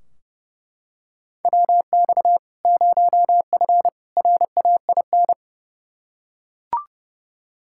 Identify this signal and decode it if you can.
Morse code, heard as a keyed tone, about 30 wpm, 700 Hz. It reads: WX 0F RAIN
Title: Practice QSO - Exchange 2 - Info - 30wpm